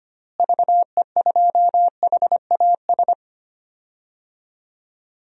Morse code, heard as a keyed tone, about 25 wpm, 700 Hz. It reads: VE2HAS
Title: DXCC-100 Call Signs - 6 Characters - Repeated 3x - 25wpm